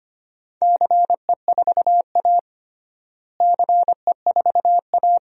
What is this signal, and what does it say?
Morse code, heard as a keyed tone, about 25 wpm, 700 Hz. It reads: CE4A CE4A
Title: DXCC-100 Call Signs - 4 Characters - Repeated 2x - 25wpm